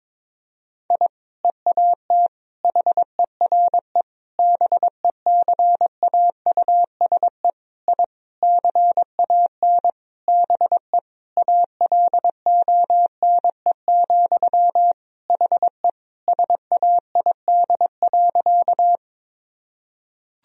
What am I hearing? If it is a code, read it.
Morse code, heard as a keyed tone, about 22 wpm, 700 Hz. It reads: I EAT HERE BECAUSE I CAN BE ALONE, HE SAID.
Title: Sentences from Top 1000 Words - Review Part 3 - 22wpm